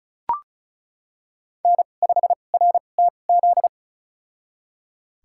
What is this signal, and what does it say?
Morse code, heard as a keyed tone, about 34 wpm, spaced as for 25 wpm, 700 Hz. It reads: N5RTZ
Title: USA Call Signs - 2x Character Spacing - 35wpm